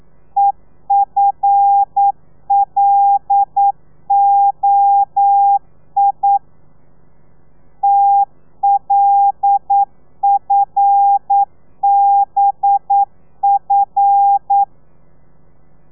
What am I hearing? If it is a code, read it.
Morse code, heard as a keyed tone, about 9 wpm, 800 Hz. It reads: EFLOI TLFBF
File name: telegrafia / i2tfj / Lez5 40